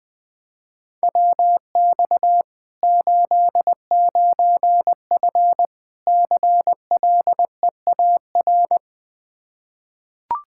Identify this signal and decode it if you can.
Morse code, heard as a keyed tone, about 19 wpm, 700 Hz. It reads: WX 89F CLEAR
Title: Practice QSO - Exchange 2 - Info - 20wpm